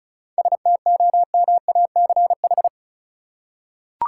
Morse code, heard as a keyed tone, about 36 wpm, 700 Hz. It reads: STOMACH
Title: Seven Letter Words - 35wpm